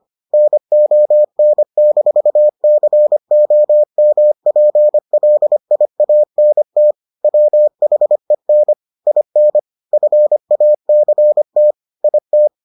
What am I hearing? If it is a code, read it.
Morse code, heard as a keyed tone, about 25 wpm, 600 Hz. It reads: NON-COMPLIANT WHEN IN FACT IT